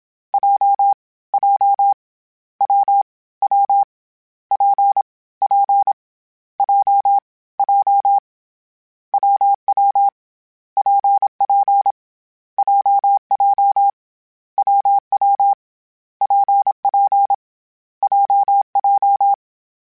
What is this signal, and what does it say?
Morse code, heard as a keyed tone, about 27 wpm, 800 Hz. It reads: J J W W P P J J WW PP JJ WW PP JJ
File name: Variable Speed / Variable Speed-mp3s / WPJ WPJ1 sp @25